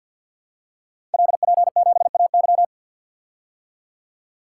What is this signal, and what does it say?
Morse code, heard as a keyed tone, about 52 wpm, 700 Hz. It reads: LP7AY
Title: Code Groups - 5 Letter:Numbers - 50wpm